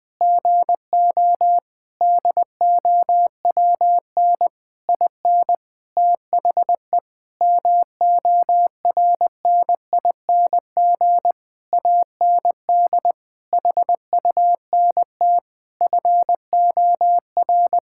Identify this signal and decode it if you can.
Morse code, heard as a keyed tone, about 20 wpm, 700 Hz. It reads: GO DOWN IN THE MORNING AND HUNT FOR
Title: Sentences from Top 1600 Words 20wpm